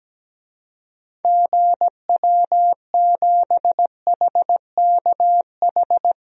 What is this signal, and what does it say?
Morse code, heard as a keyed tone, about 17 wpm, 700 Hz. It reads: GW7HKH
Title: DXCC-100 Call Signs - 6 Characters - 17wpm